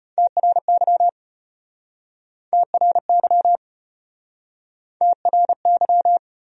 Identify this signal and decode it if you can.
Morse code, heard as a keyed tone, about 34 wpm, 700 Hz. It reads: TRY TRY TRY